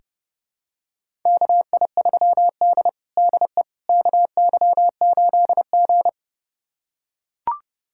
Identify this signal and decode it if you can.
Morse code, heard as a keyed tone, about 30 wpm, 700 Hz. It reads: KI3D DE KY8G
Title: USA Callsign DE Callsign - 4 Characters - 30wpm